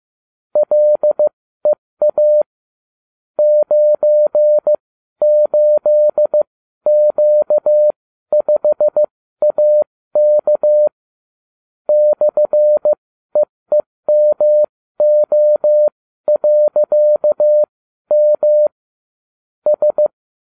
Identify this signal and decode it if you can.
Morse code, heard as a keyed tone, about 15 wpm, spaced as for 12 wpm, 600 Hz. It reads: LEA 98Q5AK /EEMO.M S
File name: cw / ubacw32 2